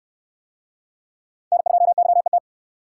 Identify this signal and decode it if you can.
Morse code, heard as a keyed tone, about 69 wpm, 700 Hz. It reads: N2YET